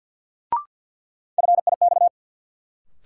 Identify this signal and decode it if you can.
Morse code, heard as a keyed tone, about 52 wpm, 700 Hz. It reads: FIX
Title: Sentences from Top 1400 Words - New Word Drill - 50wpm